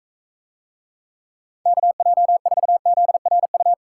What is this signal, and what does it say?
Morse code, heard as a keyed tone, about 43 wpm, 700 Hz. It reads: KJ4ZRU